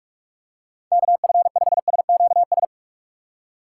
Morse code, heard as a keyed tone, about 46 wpm, 700 Hz. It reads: KF5SQS